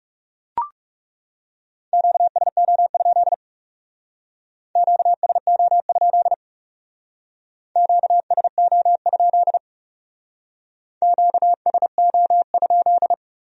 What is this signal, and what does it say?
Morse code, heard as a keyed tone, about 38 wpm, 700 Hz. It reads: QSO? QSO? QSO? QSO?